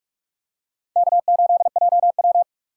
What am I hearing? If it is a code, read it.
Morse code, heard as a keyed tone, about 45 wpm, 700 Hz. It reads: K8JW
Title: USA Call Signs - 4 Characters - 2x Character Spacing - 45wpm